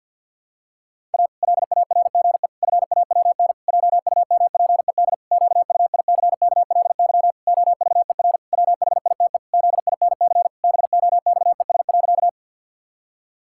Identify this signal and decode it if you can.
Morse code, heard as a keyed tone, about 51 wpm, 700 Hz. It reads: A LARGE FAWN JUMPED QUICKLY OVER WHITE ZINC BOXES.
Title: Pangram Challenge v2 - 50wpm